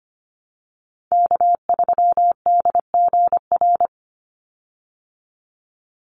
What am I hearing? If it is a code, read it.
Morse code, heard as a keyed tone, about 25 wpm, 700 Hz. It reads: K3DGR
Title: Call Signs Repeated 2x - 25wpm